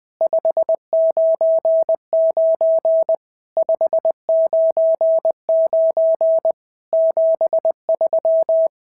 Morse code, heard as a keyed tone, about 19 wpm, 650 Hz. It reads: 599 599 73
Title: SOTA QSO Practice 20wpm